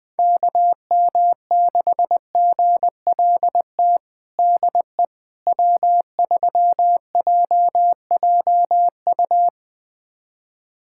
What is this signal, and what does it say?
Morse code, heard as a keyed tone, about 21 wpm, 700 Hz. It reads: KM6GLT DE W3JJU
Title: USA Callsign DE Callsign 20wpm